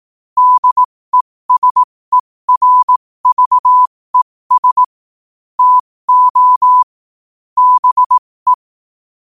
Morse code, heard as a keyed tone, about 18 wpm, spaced as for 15 wpm, 1000 Hz. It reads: DESERVES TO BE
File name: qotdfiles / 15wpm